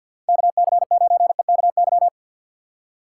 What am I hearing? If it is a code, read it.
Morse code, heard as a keyed tone, about 47 wpm, 700 Hz. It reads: KC9EKY
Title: USA Call Signs - 6 Characters - 6x Character Spacing - 50wpm